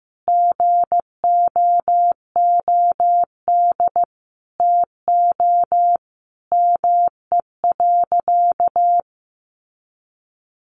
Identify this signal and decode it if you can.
Morse code, heard as a keyed tone, about 15 wpm, 700 Hz. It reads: GOOD TO ME.
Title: Sentences from Top 1100 Words 15wpm